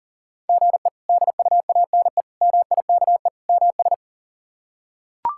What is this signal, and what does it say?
Morse code, heard as a keyed tone, about 37 wpm, 700 Hz. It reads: GE DUANE MIKE MS
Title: K1USN SST QSO Practice - 40wpm